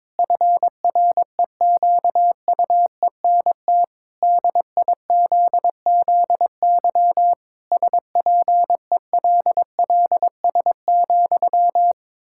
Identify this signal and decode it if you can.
Morse code, heard as a keyed tone, about 22 wpm, 700 Hz. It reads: FREQUENT DIZZY SPELLS,
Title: Pangram Challenge v2 - 22wpm